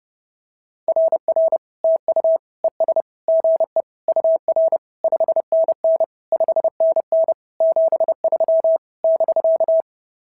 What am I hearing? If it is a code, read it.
Morse code, heard as a keyed tone, about 30 wpm, 650 Hz. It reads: RR TU ES GE UR 5NN 5NN 73 <BK>
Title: SOTA QSO Practice 30wpm